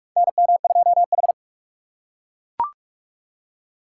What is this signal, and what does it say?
Morse code, heard as a keyed tone, about 41 wpm, 700 Hz. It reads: NM2H
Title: USA Call Signs - 4 Characters - 3x Character Spacing - 45wpm